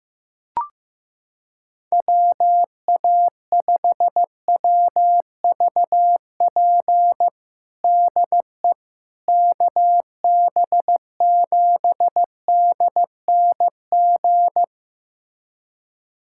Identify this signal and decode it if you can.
Morse code, heard as a keyed tone, about 15 wpm, 700 Hz. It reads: WA5WVP DE KB7DNG
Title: USA Callsign DE Callsign - 6 Characters - 15wpm